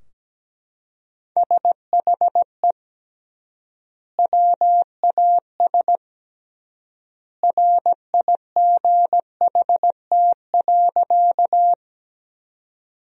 Morse code, heard as a keyed tone, about 17 wpm, 700 Hz. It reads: SHE WAS RIGHT.
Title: 3x Word-Spacing - Sentences from Top 200 Words - 17wpm